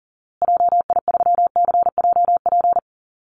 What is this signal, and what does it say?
Morse code, heard as a keyed tone, about 40 wpm, 700 Hz. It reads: JI3CJP
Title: Call Signs Repeated 2x - 40wpm